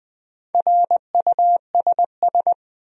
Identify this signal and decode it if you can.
Morse code, heard as a keyed tone, about 20 wpm, 700 Hz. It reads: RUSS
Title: USA Common Names - Max 4 Letters - 20wpm